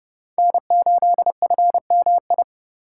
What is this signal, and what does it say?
Morse code, heard as a keyed tone, about 30 wpm, 700 Hz. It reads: N8FMI
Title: USA Call Signs - 5 Characters - 4x Character Spacing - 30wpm